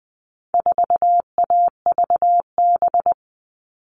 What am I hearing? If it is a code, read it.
Morse code, heard as a keyed tone, about 20 wpm, 700 Hz. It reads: 4AVB
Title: All Cut Numbers - 4-Digit 20wpm